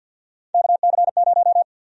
Morse code, heard as a keyed tone, about 52 wpm, 700 Hz. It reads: KC0
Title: USA Call Sign Prefixes - Repeated 2x - 50wpm